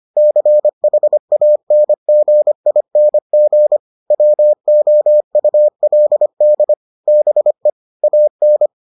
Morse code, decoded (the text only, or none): CHANGING WOULD BE AN